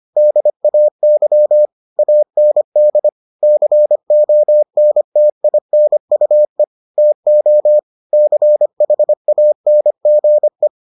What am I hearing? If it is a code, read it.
Morse code, heard as a keyed tone, about 25 wpm, 600 Hz. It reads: DAY AND CONTINUE TO CHANGE